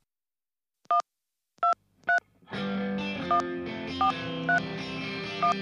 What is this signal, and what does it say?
DTMF, heard as telephone touch tones, tones of about 120 ms. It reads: 1231431